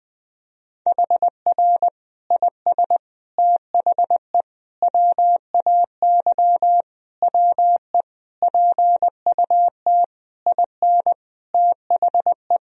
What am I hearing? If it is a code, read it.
Morse code, heard as a keyed tone, about 20 wpm, 700 Hz. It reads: HR IS THE WAY WE PUT IN THE
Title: Sentences from Top 200 Words and Abbreviations 20wpm